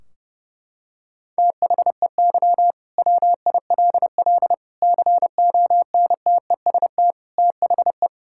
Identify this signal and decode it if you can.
Morse code, heard as a keyed tone, about 30 wpm, 700 Hz. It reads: THEY WILL CONTEST THE
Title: Sentences from Top 1600 Words 30wpm